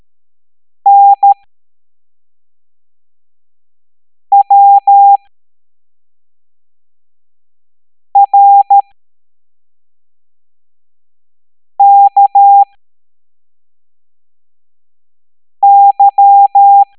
Morse code, heard as a keyed tone, about 13 wpm, spaced as for 3 wpm, 800 Hz. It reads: NWRKY